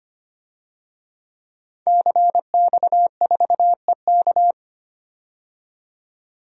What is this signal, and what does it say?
Morse code, heard as a keyed tone, about 25 wpm, 700 Hz. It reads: CX4EK
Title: DXCC-100 Call Signs - 5 Characters - 25wpm